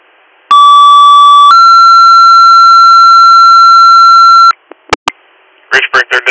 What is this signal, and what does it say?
Two-tone page, heard as a one-second tone, then a three-second tone, about 1150 then 1360 Hz, with speech. It is heard from 0.5 to 4.5 s.